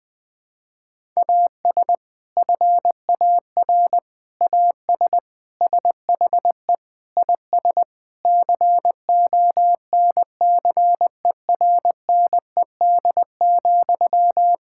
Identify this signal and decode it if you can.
Morse code, heard as a keyed tone, about 20 wpm, 700 Hz. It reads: AS FAR AS SHE IS CONCERNED,
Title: Sentences from Top 1000 Words - Review Part 6 - 20wpm